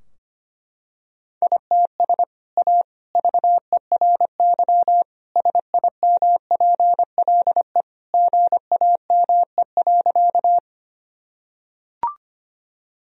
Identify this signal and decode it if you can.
Morse code, heard as a keyed tone, about 25 wpm, 700 Hz. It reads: ITS A VERY SIMPLE GAME.
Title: Sentences from Top 1000 Words - Review Part 3 - 25wpm